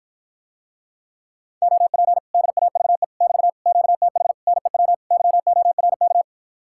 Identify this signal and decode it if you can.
Morse code, heard as a keyed tone, about 52 wpm, 700 Hz. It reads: OP DAVE <BT> QTH NEW YORK